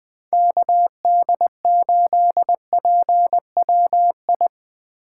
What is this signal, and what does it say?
Morse code, heard as a keyed tone, about 22 wpm, 700 Hz. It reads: KD8PWI
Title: Speed Racing - USA 2x3 Call Signs 20wpm